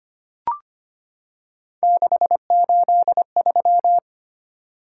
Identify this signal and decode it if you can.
Morse code, heard as a keyed tone, about 25 wpm, 700 Hz. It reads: 683
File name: Three Letter-Number Combinations - 25wpm